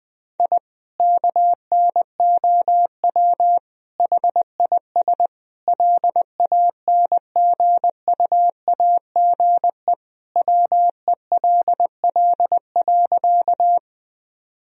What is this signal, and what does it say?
Morse code, heard as a keyed tone, about 21 wpm, 700 Hz. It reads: I KNOW HIS LANGUAGE WELL.
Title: Sentences from Top 1000 Words - Review Part 3 - 20wpm